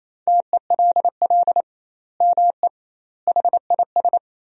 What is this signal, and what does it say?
Morse code, heard as a keyed tone, about 28 wpm, 700 Hz. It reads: TELL ME HIS
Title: 2x Word-Spacing - Sentences from Top 200 Words - 28wpm